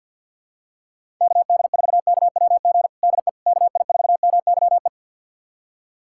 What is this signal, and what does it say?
Morse code, heard as a keyed tone, about 50 wpm, 700 Hz. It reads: KD4KWG DE KI4MYE
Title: USA Callsign DE Callsign 50wpm